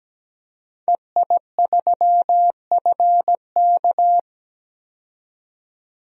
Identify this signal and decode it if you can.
Morse code, heard as a keyed tone, about 17 wpm, 700 Hz. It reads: EI3FK
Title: DXCC-100 Call Signs - 5 Characters - 17wpm